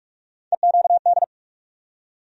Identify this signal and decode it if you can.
Morse code, heard as a keyed tone, about 43 wpm, 700 Hz. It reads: EQD
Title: USA Call Sign Postfixes - 3 Letters 45wpm